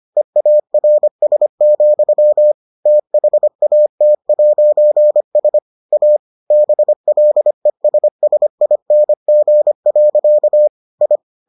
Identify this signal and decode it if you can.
Morse code, heard as a keyed tone, about 25 wpm, 600 Hz. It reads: EARS, THAT'S A BLESSING. I